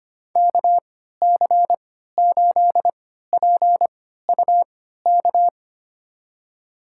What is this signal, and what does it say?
Morse code, heard as a keyed tone, about 25 wpm, spaced as for 14 wpm, 700 Hz. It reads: KC8PUK
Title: USA Call Signs - 6 Characters - 3x Character Spacing - 25wpm